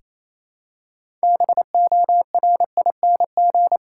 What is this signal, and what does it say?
Morse code, heard as a keyed tone, about 28 wpm, 700 Hz. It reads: BORING
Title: Six Letter Words - 28wpm